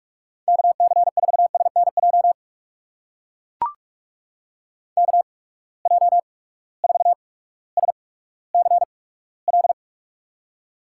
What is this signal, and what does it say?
Morse code, heard as a keyed tone, about 43 wpm, spaced as for 19 wpm, 700 Hz. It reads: KC4SNJ KJ4SCL
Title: USA Call Signs - 6 Characters - 8x Character Spacing - 45wpm